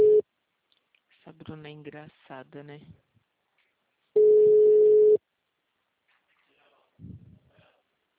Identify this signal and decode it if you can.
Ringback tone (Europe, 425 Hz), with speech.